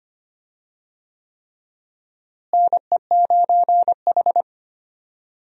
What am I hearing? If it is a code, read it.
Morse code, heard as a keyed tone, about 25 wpm, 700 Hz. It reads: NE9H
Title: USA Call Signs - 4 Characters - Repeated 3x - 25wpm